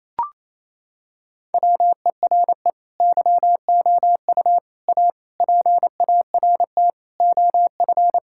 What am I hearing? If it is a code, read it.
Morse code, heard as a keyed tone, about 27 wpm, 700 Hz. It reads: WERE YOU A PART OF
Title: Sentences from Top 1000 Words - Review Part 6 - 28wpm